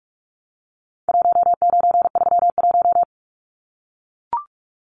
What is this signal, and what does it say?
Morse code, heard as a keyed tone, about 44 wpm, 700 Hz. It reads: 1931
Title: Numbers - 4-Digit 45wpm